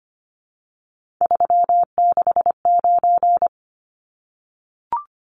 Morse code, heard as a keyed tone, about 25 wpm, 700 Hz. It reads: 369
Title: Numbers - 3-Digit 25wpm